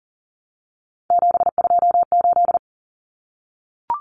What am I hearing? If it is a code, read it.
Morse code, heard as a keyed tone, about 40 wpm, 700 Hz. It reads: 728
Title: Common Cut Numbers - 3-Digit 40wpm